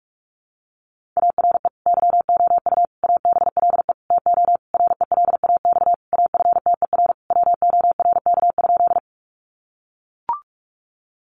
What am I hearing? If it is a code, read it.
Morse code, heard as a keyed tone, about 44 wpm, 700 Hz. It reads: ARE YOU ABLE TO RELAX AFTER WORK?
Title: Sentences from Top 1400 Words 45wpm